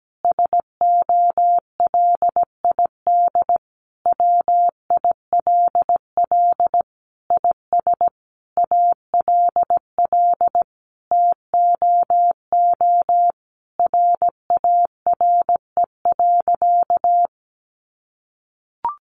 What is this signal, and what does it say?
Morse code, heard as a keyed tone, about 17 wpm, 700 Hz. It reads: SOLID WILL IS ALL TOO RARE.
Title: Sentences from Top 1500 Words 17wpm